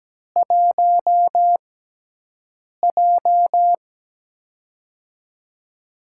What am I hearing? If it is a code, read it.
Morse code, heard as a keyed tone, about 17 wpm, spaced as for 6 wpm, 700 Hz. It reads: 1J
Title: USA Call Signs - 4 Characters - 6x Character Spacing - 17wpm